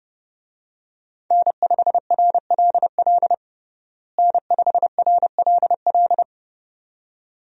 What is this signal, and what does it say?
Morse code, heard as a keyed tone, about 30 wpm, 700 Hz. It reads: N5RLL N5RLL
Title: USA Call Signs - 5 Characters - Repeated 2x - 30wpm